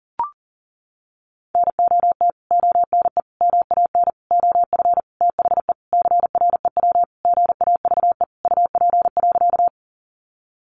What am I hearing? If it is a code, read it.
Morse code, heard as a keyed tone, about 40 wpm, 700 Hz. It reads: NOT ONE MAN OF THE CREW GAVE UP.
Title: Sentences from Top 1500 Words 40wpm